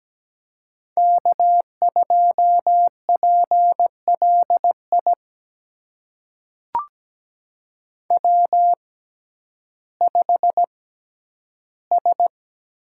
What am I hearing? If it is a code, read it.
Morse code, heard as a keyed tone, about 17 wpm, spaced as for 10 wpm, 700 Hz. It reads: K2PLI W5S
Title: USA Call Signs - 5 Characters - 6x Character Spacing - 17wpm